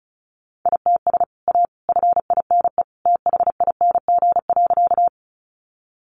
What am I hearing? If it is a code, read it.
Morse code, heard as a keyed tone, about 34 wpm, 700 Hz. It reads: ITS A FINE THING.
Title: Sentences from Top 600 Words 35wpm